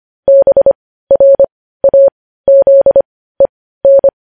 Morse code, heard as a keyed tone, about 25 wpm, spaced as for 15 wpm, 550 Hz. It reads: BRAZEN
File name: podcast / News headlines-1x-25x15